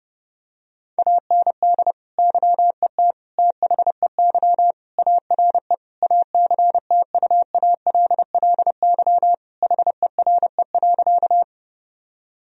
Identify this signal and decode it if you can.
Morse code, heard as a keyed tone, about 32 wpm, 700 Hz. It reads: AND YET THEY ARE ACTUALLY HERE.
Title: Sentences from Top 1000 Words - Review Part 4 - 30wpm